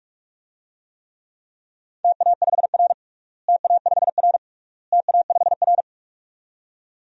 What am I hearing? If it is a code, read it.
Morse code, heard as a keyed tone, about 46 wpm, 700 Hz. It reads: TA5R TA5R TA5R
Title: DXCC-100 Call Signs - 4 Characters - Repeated 3x - 45wpm